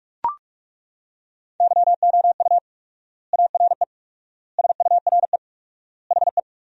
Morse code, heard as a keyed tone, about 43 wpm, 700 Hz. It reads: YOU ARE SURE HE
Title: 4x Word-Spacing - Sentences from Top 300 Words - 45wpm